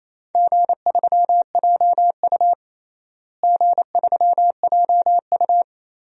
Morse code, heard as a keyed tone, about 28 wpm, 700 Hz. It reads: G3JU G3JU
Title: DXCC-100 Call Signs - 4 Characters - Repeated 2x - 28wpm